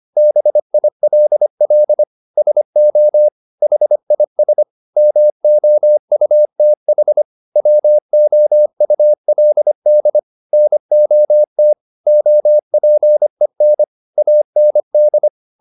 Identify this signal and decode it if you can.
Morse code, heard as a keyed tone, about 25 wpm, 600 Hz. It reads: BILL SO HIS MOUTH WOULD NOT OPEN AND